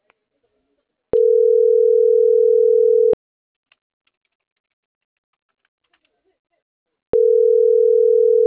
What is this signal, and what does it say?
Ringback tone (North America), 440 + 480 Hz.